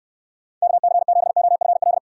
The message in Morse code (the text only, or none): KC7OFL